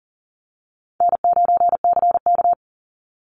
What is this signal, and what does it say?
Morse code, heard as a keyed tone, about 40 wpm, 700 Hz. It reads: N9CK
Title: Easy Call Signs - 40wpm